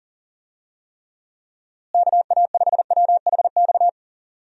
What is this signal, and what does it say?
Morse code, heard as a keyed tone, about 40 wpm, 700 Hz. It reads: KA5WHX
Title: USA Call Signs - 6 Characters - Repeated 3x - 40wpm